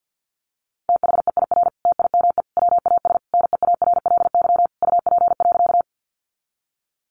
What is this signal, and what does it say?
Morse code, heard as a keyed tone, about 50 wpm, 700 Hz. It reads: THEIR TIME WAS NEARLY UP.
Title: Sentences from Top 1300 Words 50wpm